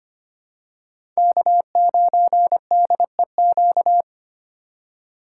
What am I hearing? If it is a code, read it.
Morse code, heard as a keyed tone, about 25 wpm, 700 Hz. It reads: K9DEQ
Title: USA Call Signs - 5 Characters - 3x Character Spacing - 25wpm